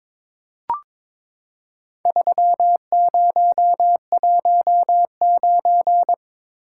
Morse code, heard as a keyed tone, about 22 wpm, 700 Hz. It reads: 3019
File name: Four Letter-Number Combinations - 22wpm